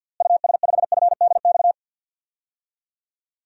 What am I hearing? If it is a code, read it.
Morse code, heard as a keyed tone, about 51 wpm, 700 Hz. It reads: US5FDX
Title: DXCC-100 Call Signs - 6 Characters - Repeated 2x - 50wpm